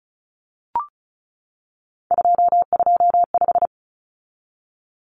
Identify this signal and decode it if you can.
Morse code, heard as a keyed tone, about 34 wpm, 700 Hz. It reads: 225
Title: Common Cut Numbers - 3-Digit 35wpm